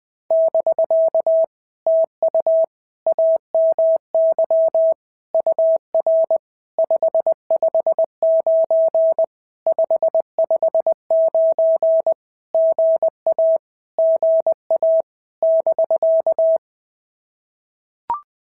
Morse code, heard as a keyed tone, about 20 wpm, 650 Hz. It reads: <BK> TU AMY UR 559 559 GA GA <BK>
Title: POTA QSO Practice - 20wpm